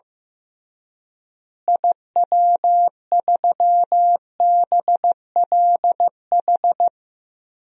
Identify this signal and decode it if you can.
Morse code, heard as a keyed tone, about 15 wpm, 700 Hz. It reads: IW3BLH
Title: DXCC-100 Call Signs - 6 Characters - Repeated 3x - 15wpm